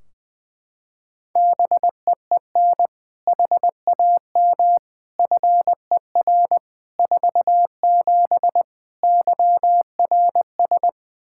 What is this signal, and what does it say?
Morse code, heard as a keyed tone, about 20 wpm, 700 Hz. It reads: BEEN HAM FER 47 YRS
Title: Practice QSO - Exchange 3 - More Info - 20wpm